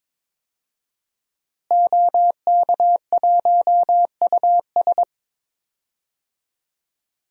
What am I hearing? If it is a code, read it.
Morse code, heard as a keyed tone, about 22 wpm, 700 Hz. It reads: OK1US
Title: DXCC-100 Call Signs - 5 Characters - Repeated 3x - 22wpm